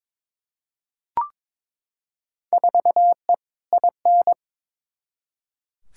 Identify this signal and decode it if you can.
Morse code, heard as a keyed tone, about 22 wpm, 700 Hz. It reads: 4E IN
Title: Field Day Exchanges - 22wpm